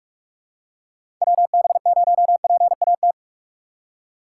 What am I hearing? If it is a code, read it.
Morse code, heard as a keyed tone, about 42 wpm, 700 Hz. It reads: WB0PAT
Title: USA Call Signs - 6 Characters - 6x Character Spacing - 45wpm